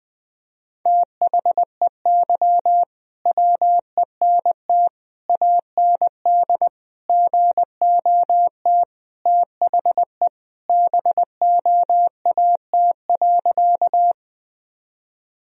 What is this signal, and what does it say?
Morse code, heard as a keyed tone, about 21 wpm, 700 Hz. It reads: THEY WENT AND GOT THE BOAT.
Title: Sentences from Top 1000 Words - Review Part 3 - 20wpm